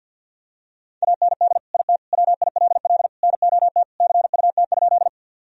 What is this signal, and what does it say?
Morse code, heard as a keyed tone, about 46 wpm, 700 Hz. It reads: AND IT WILL NOT CUT?
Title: Sentences from Top 1000 Words - Review Part 1 - 50wpm